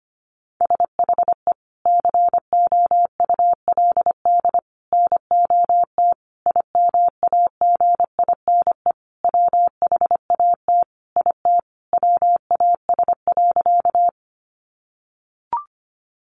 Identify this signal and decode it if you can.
Morse code, heard as a keyed tone, about 25 wpm, 700 Hz. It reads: SHE COULD NOT IMAGINE WHAT IT WAS.
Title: Sentences from Top 1400 Words 25wpm